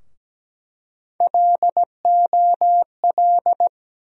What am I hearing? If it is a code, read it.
Morse code, heard as a keyed tone, about 17 wpm, 700 Hz. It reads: LOL
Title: USA Call Sign Postfixes - 3 Letters 17wpm